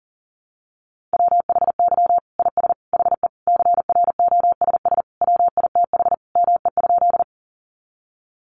Morse code, heard as a keyed tone, about 40 wpm, 700 Hz. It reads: WHY IS HE CROSS WITH ME?
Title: Sentences from Top 300 Words 40wpm